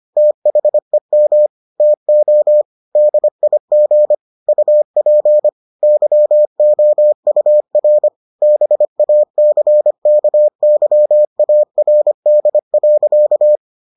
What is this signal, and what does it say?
Morse code, heard as a keyed tone, about 25 wpm, 600 Hz. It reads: THEM TO DIG UP YOUR BACKYARD.